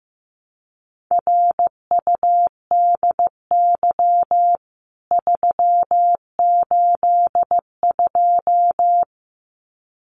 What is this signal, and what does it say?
Morse code, heard as a keyed tone, about 15 wpm, 700 Hz. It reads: RUDY 382
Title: CWT Exchange 15wpm